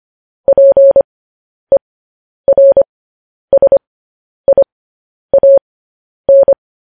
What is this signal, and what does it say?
Morse code, heard as a keyed tone, about 25 wpm, spaced as for 10 wpm, 550 Hz. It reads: PERSIAN
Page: news headlines in morse code at 10 wpm